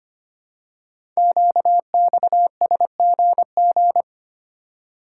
Morse code, heard as a keyed tone, about 25 wpm, 700 Hz. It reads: QXSGG
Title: Code Groups - 5 Letter:Numbers - 25wpm